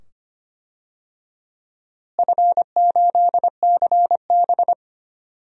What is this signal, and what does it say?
Morse code, heard as a keyed tone, about 25 wpm, 700 Hz. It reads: F8CB
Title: DXCC-100 Call Signs - 4 Characters - Repeated 2x - 25wpm